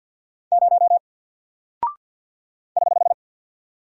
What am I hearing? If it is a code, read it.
Morse code, heard as a keyed tone, about 48 wpm, 700 Hz. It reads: 0 <HH>
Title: Single Character (Letter:Number:Punctuation:Prosign) 50wpm